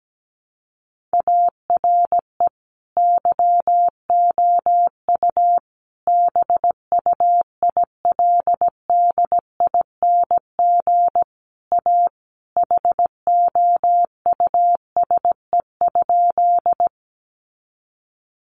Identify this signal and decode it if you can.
Morse code, heard as a keyed tone, about 17 wpm, 700 Hz. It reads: ARE YOU BUILDING A HOUSE?
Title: Sentences from Top 700 Words 17wpm